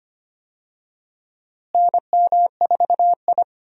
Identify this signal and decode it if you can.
Morse code, heard as a keyed tone, about 25 wpm, 700 Hz. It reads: NM4I
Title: USA Call Signs - 4 Characters - Repeated 3x - 25wpm